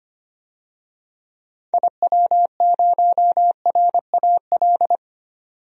Morse code, heard as a keyed tone, about 25 wpm, 700 Hz. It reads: IW0RAL
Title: DXCC-100 Call Signs - 6 Characters - 25wpm